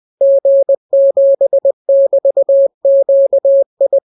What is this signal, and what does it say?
Morse code, heard as a keyed tone, about 20 wpm, 550 Hz. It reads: G7<BT>QI